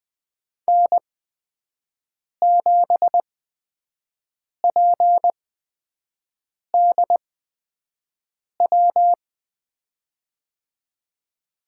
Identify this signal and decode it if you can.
Morse code, heard as a keyed tone, about 20 wpm, spaced as for 5 wpm, 700 Hz. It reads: N7PDW
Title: USA Call Signs - 5 Characters - 8x Character Spacing - 20wpm